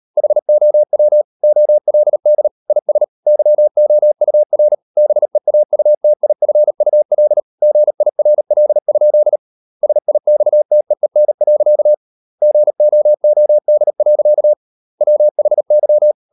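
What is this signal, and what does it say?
Morse code, heard as a keyed tone, about 38 wpm, 600 Hz. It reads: HOW OLD IS YOUR BEAUTIFUL GIRL? SIXTEEN. GOOD. WHY
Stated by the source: Morse code practice at 38 WPM, sending Ken Follett's novel Never in English, from CW med Gnister.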